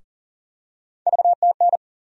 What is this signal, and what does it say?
Morse code, heard as a keyed tone, about 38 wpm, 700 Hz. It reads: VTN
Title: USA Call Sign Postfixes - 3 Letters 40wpm